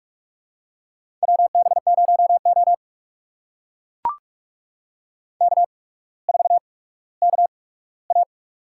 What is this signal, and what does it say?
Morse code, heard as a keyed tone, about 44 wpm, spaced as for 19 wpm, 700 Hz. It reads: WB0O K4KA
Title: USA Call Signs - 4 Characters - 8x Character Spacing - 45wpm